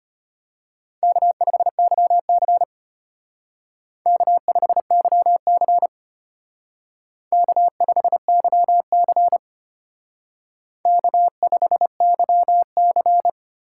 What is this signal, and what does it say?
Morse code, heard as a keyed tone, about 32 wpm, 700 Hz. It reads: K5YC K5YC K5YC K5YC